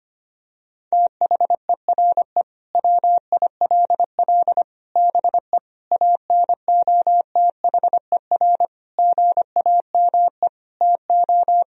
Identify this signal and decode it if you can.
Morse code, heard as a keyed tone, about 25 wpm, 700 Hz. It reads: THERE WILL BE ANOTHER GAME TO
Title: Sentences from Top 1000 Words - Review Part 3 - 25wpm